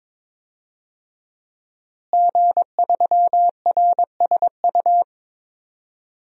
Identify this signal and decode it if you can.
Morse code, heard as a keyed tone, about 22 wpm, 700 Hz. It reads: G3RSU
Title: DXCC-100 Call Signs - 5 Characters - Repeated 2x - 22wpm